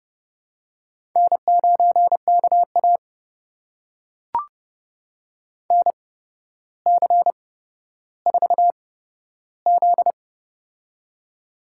Morse code, heard as a keyed tone, about 30 wpm, spaced as for 13 wpm, 700 Hz. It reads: N9KA NC4Z
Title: USA Call Signs - 4 Characters - 8x Character Spacing - 30wpm